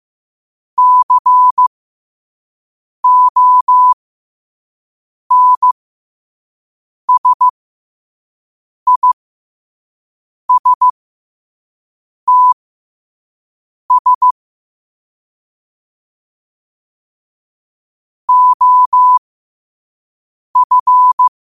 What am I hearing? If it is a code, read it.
Morse code, heard as a keyed tone, about 15 wpm, spaced as for 5 wpm, 1000 Hz. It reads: CONSISTS OF
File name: qotdfiles / 5wpm